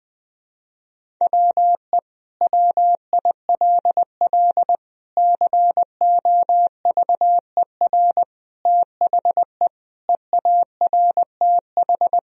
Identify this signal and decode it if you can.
Morse code, heard as a keyed tone, about 19 wpm, 700 Hz. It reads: WE WILL COVER THE EARTH
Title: Sentences from Top 1000 Words - Review Part 1 - 20wpm